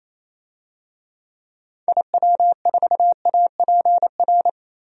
Morse code, heard as a keyed tone, about 28 wpm, 700 Hz. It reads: IW4APR
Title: DXCC-100 Call Signs - 6 Characters - 28wpm